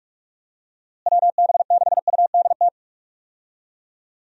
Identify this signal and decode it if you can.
Morse code, heard as a keyed tone, about 41 wpm, 700 Hz. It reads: WB6UDT
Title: USA Call Signs - 6 Characters - Repeated 3x - 45wpm